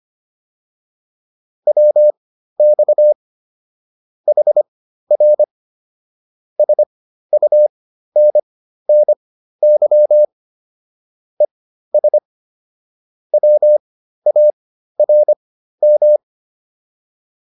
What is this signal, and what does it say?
Morse code, heard as a keyed tone, about 25 wpm, spaced as for 13 wpm, 600 Hz. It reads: WX HR SUNNY ES WARM
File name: QSO303 13